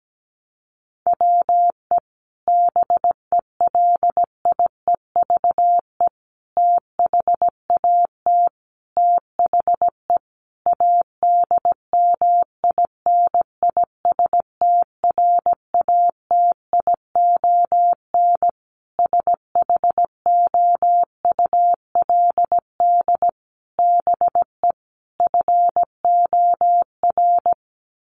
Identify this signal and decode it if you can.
Morse code, heard as a keyed tone, about 17 wpm, 700 Hz. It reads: WE BELIEVE THAT THE ADMINISTRATION SHOULD BE FOR
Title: Sentences from Top 1300 Words 17wpm